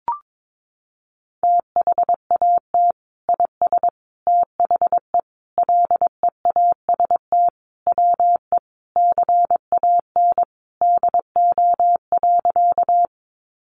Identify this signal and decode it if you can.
Morse code, heard as a keyed tone, about 22 wpm, 700 Hz. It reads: THAT IS THE LEAST WE CAN DO.
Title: Sentences from Top 700 Words 22wpm